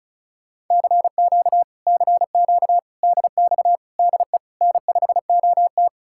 Morse code, heard as a keyed tone, about 46 wpm, spaced as for 41 wpm, 700 Hz. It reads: CQ CQ DX DE N5OT
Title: Speed-Racing Common QSO Phrases - 35wpm